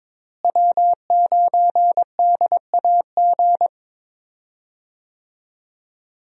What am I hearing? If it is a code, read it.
Morse code, heard as a keyed tone, about 22 wpm, 700 Hz. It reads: W9DAG
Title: USA Call Signs - 5 Characters - Repeated 3x - 22wpm